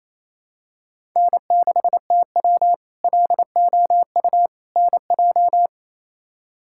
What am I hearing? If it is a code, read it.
Morse code, heard as a keyed tone, about 28 wpm, 700 Hz. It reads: N6TW LOU NJ
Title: K1USN SST QSO Practice - 28wpm